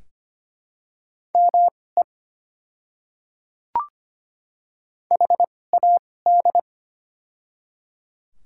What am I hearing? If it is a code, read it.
Morse code, heard as a keyed tone, about 25 wpm, spaced as for 18 wpm, 690 Hz. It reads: ME HAD
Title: 2x Character Spacing - Top 200 Words - 25wpm